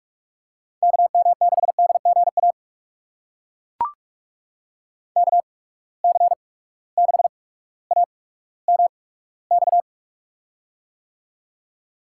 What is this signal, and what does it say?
Morse code, heard as a keyed tone, about 43 wpm, spaced as for 19 wpm, 700 Hz. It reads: KM6DGA KC6AMX
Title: USA Call Signs - 6 Characters - 8x Character Spacing - 45wpm